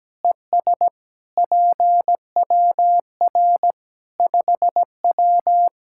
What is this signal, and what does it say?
Morse code, heard as a keyed tone, about 17 wpm, 700 Hz. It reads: ES PWR 5W
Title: Practice QSO - Exchange 2 - Info - 17wpm